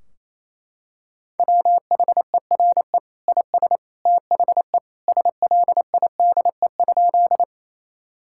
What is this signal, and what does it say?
Morse code, heard as a keyed tone, about 27 wpm, 700 Hz. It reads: WHERE IS THE SLIDE?
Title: Sentences from Top 1600 Words 28wpm